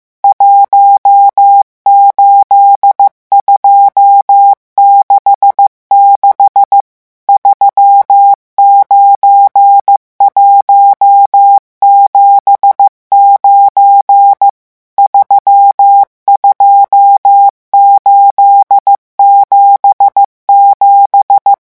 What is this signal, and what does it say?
Morse code, heard as a keyed tone, about 15 wpm, 800 Hz. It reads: 18266 39179 32877
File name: morse mp3 / revision11 chiffres vitesse 14